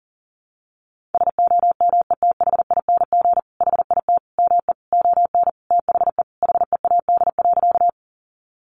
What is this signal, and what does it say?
Morse code, heard as a keyed tone, about 40 wpm, 700 Hz. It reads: SOMETHING HIT ME ON THE HEAD.
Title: Sentences from Top 1200 Words 40wpm